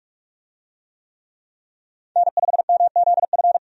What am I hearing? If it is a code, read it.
Morse code, heard as a keyed tone, about 46 wpm, 700 Hz. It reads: N5MZF
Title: USA Call Signs - 5 Characters - 45wpm